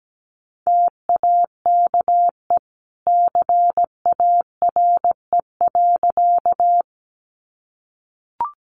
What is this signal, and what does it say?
Morse code, heard as a keyed tone, about 17 wpm, 700 Hz. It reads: TAKE CARE.
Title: Sentences from Top 300 Words 17wpm